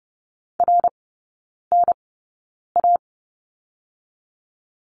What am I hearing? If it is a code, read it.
Morse code, heard as a keyed tone, about 30 wpm, spaced as for 9 wpm, 690 Hz. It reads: RNA
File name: Lesson 17e - Letter u - Three Characters 30wpm